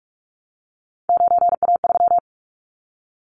Morse code, heard as a keyed tone, about 44 wpm, 700 Hz. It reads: 9A3
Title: Callsign Prefixes for Top 100 DXCC - 45wpm